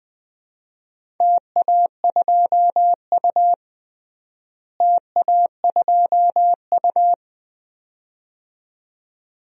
Morse code, heard as a keyed tone, about 20 wpm, 700 Hz. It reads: TA2U TA2U
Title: DXCC-100 Call Signs - 4 Characters - Repeated 2x - 20wpm